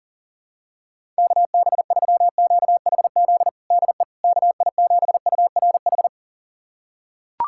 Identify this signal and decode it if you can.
Morse code, heard as a keyed tone, about 37 wpm, 700 Hz. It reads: KB3QHZ DE KI7URH
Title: USA Callsign DE Callsign - 6 Characters - 40wpm